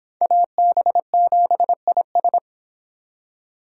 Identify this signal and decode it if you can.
Morse code, heard as a keyed tone, about 25 wpm, 700 Hz. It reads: AB7IS